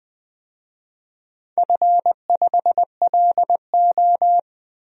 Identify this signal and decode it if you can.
Morse code, heard as a keyed tone, about 20 wpm, 700 Hz. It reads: F5LO
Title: DXCC-100 Call Signs - 4 Characters - 20wpm